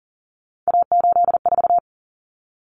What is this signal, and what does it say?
Morse code, heard as a keyed tone, about 40 wpm, 700 Hz. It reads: A84
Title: Common Cut Numbers - 3-Digit 40wpm